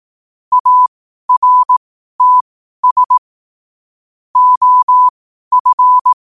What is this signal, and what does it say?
Morse code, heard as a keyed tone, about 18 wpm, spaced as for 12 wpm, 1000 Hz. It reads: ARTS OF
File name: 12wpm